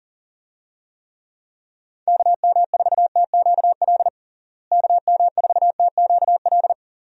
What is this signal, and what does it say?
Morse code, heard as a keyed tone, about 40 wpm, 700 Hz. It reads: KM4TQL KM4TQL
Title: USA Call Signs - 6 Characters - Repeated 3x - 40wpm